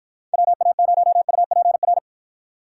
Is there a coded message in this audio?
Yes: Morse code from 0.3 to 2.0 s, heard as a keyed tone, about 47 wpm, 700 Hz. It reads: WA0UPR